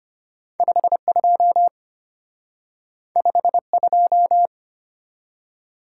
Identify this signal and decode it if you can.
Morse code, heard as a keyed tone, about 33 wpm, 700 Hz. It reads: 52 52